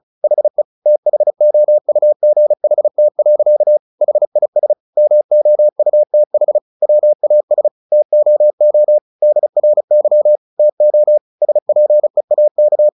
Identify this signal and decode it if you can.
Morse code, heard as a keyed tone, about 34 wpm, 600 Hz. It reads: HE THOUGHT. HIS MOUTH WAS TOO DRY TO SPEAK
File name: the old man and the sea 2 35wpm